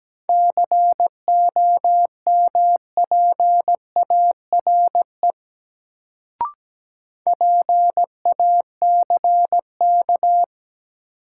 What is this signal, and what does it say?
Morse code, heard as a keyed tone, about 17 wpm, 700 Hz. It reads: COMPARE PACK
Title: Sentences from Top 1300 Words - New Word Drill Alt - 17wpm